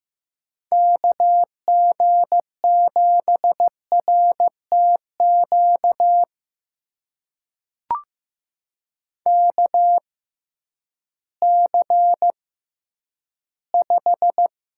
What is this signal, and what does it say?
Morse code, heard as a keyed tone, about 15 wpm, spaced as for 10 wpm, 700 Hz. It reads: KG7RTQ KC5
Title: USA Call Signs - 6 Characters - 6x Character Spacing - 15wpm